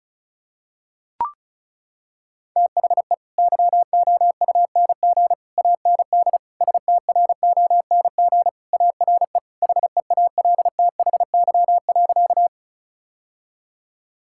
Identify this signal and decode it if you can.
Morse code, heard as a keyed tone, about 36 wpm, 700 Hz. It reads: THE YOUNG AND STRONG ARE HEALTHY.
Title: Sentences from Top 1000 Words - Review Part 5 - 35wpm